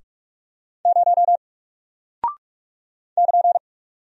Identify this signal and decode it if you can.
Morse code, heard as a keyed tone, about 46 wpm, 700 Hz. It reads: 0 <KN>
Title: Single Character (Letter:Number:Punctuation:Prosign) Alt 45wpm